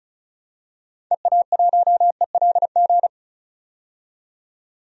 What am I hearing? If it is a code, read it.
Morse code, heard as a keyed tone, about 36 wpm, 700 Hz. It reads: EA1ELG
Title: DXCC-100 Call Signs - 6 Characters - 35wpm